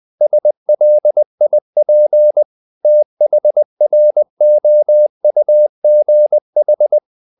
Morse code, heard as a keyed tone, about 20 wpm, 600 Hz. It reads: SLIP THROUGH
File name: the old man and the sea / the old man and the sea part 1 20wpm